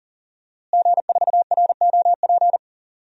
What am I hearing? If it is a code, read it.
Morse code, heard as a keyed tone, about 37 wpm, 700 Hz. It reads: G4ROP
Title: DXCC-100 Call Signs - 5 Characters - Repeated 2x - 40wpm